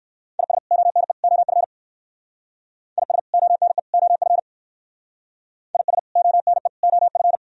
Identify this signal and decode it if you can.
Morse code, heard as a keyed tone, about 58 wpm, 700 Hz. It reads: IS ONE OF IS ONE OF IS ONE OF